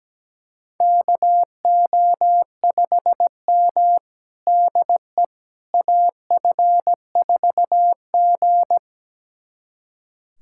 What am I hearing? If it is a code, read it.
Morse code, heard as a keyed tone, about 17 wpm, 700 Hz. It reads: KO5M DE AF4G
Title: USA Callsign DE Callsign - 4 Characters - 17wpm